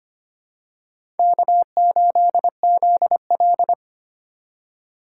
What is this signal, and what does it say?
Morse code, heard as a keyed tone, about 25 wpm, 700 Hz. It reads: K8ZL
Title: USA Call Signs - 4 Characters - 4x Character Spacing - 25wpm